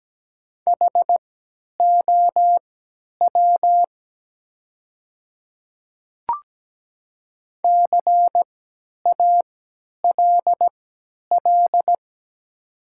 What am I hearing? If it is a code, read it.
Morse code, heard as a keyed tone, about 17 wpm, spaced as for 10 wpm, 700 Hz. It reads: HOW CALL
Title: 3x Character Spacing - Top 100 Words - 17wpm